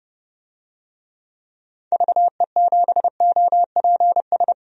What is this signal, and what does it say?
Morse code, heard as a keyed tone, about 32 wpm, 700 Hz. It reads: VE7OPS
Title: DXCC-100 Call Signs - 6 Characters - 30wpm